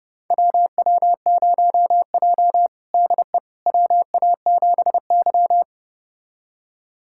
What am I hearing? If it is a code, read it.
Morse code, heard as a keyed tone, about 32 wpm, 700 Hz. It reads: WW0J DE WA7Y
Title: USA Callsign DE Callsign - 4 Characters - 30wpm